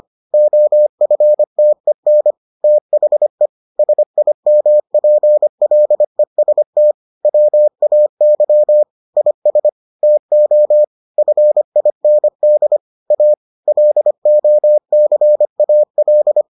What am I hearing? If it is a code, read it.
Morse code, heard as a keyed tone, about 25 wpm, 600 Hz. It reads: OFTEN THE SIMPLEST WAY IS TO FIND A LOCAL